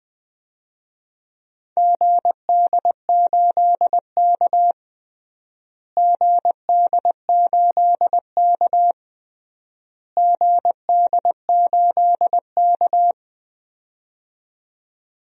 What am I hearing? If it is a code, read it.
Morse code, heard as a keyed tone, about 20 wpm, 700 Hz. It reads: GD8K GD8K GD8K
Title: DXCC-100 Call Signs - 4 Characters - Repeated 3x - 20wpm